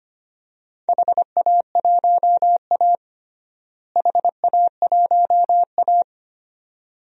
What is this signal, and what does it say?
Morse code, heard as a keyed tone, about 25 wpm, 700 Hz. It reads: HA1A HA1A
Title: DXCC-100 Call Signs - 4 Characters - Repeated 2x - 25wpm